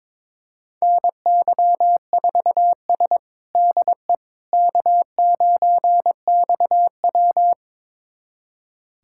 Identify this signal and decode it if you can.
Morse code, heard as a keyed tone, about 22 wpm, 700 Hz. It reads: NY4S DE K9XW
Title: USA Callsign DE Callsign - 4 Characters - 22wpm